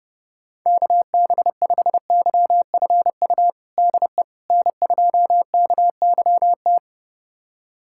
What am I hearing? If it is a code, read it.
Morse code, heard as a keyed tone, about 30 wpm, 700 Hz. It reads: KB5YFU DE N2KYT
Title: USA Callsign DE Callsign 30wpm